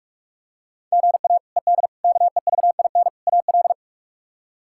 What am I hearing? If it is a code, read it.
Morse code, heard as a keyed tone, about 43 wpm, 700 Hz. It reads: GA ED KEVIN AL
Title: K1USN SST QSO Practice - 45wpm